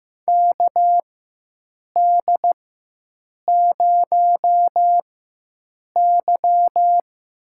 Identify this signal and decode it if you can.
Morse code, heard as a keyed tone, about 15 wpm, spaced as for 7 wpm, 700 Hz. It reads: KD0Y
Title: USA Call Signs - 6 Characters - 4x Character Spacing - 15wpm